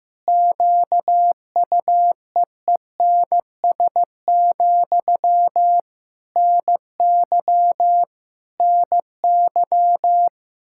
Morse code, heard as a keyed tone, about 15 wpm, 700 Hz. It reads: QUEENS, NY NY
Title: K1USN SST QSO Practice 15wpm